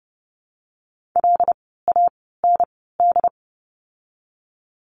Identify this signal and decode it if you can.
Morse code, heard as a keyed tone, about 30 wpm, spaced as for 17 wpm, 700 Hz. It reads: LAND
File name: Lesson 17f - Letter u - Words 30wpm